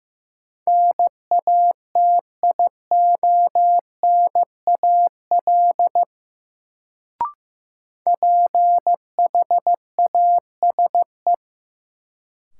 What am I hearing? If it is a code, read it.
Morse code, heard as a keyed tone, about 15 wpm, 700 Hz. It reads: NATIONAL PHASE
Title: Sentences from Top 1200 Words - New Word Drill Alt - 15wpm